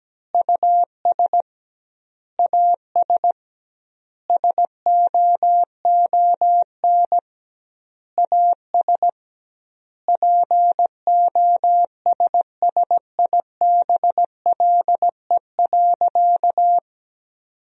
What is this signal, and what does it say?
Morse code, heard as a keyed tone, about 17 wpm, 700 Hz. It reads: US AS SOON AS POSSIBLE.
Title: 2x Word-Spacing - Sentences from Top 500 Words - 17wpm